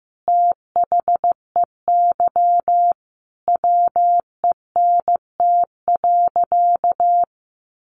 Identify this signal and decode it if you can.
Morse code, heard as a keyed tone, about 15 wpm, 700 Hz. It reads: THEY WENT.
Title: Sentences from Top 1300 Words 15wpm